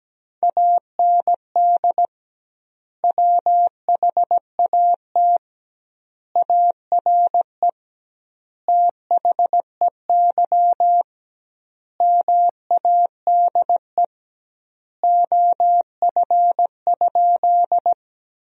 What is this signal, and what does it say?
Morse code, heard as a keyed tone, about 17 wpm, 700 Hz. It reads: AND WHAT ARE THEY MADE OF?
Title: 2x Word-Spacing - Sentences from Top 200 Words - 17wpm